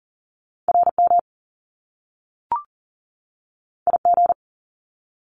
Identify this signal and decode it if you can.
Morse code, heard as a keyed tone, about 42 wpm, 700 Hz. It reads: RM IG
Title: Top Two Letter Combinations in Words 40wpm